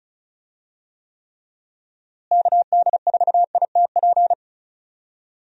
Morse code, heard as a keyed tone, about 36 wpm, 700 Hz. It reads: KD4ITP
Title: USA Call Signs - 6 Characters - Repeated 2x - 35wpm